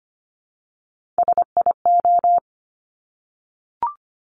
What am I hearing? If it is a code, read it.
Morse code, heard as a keyed tone, about 25 wpm, 700 Hz. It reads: SIO
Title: Top Three Letter Combinations in Words 25wpm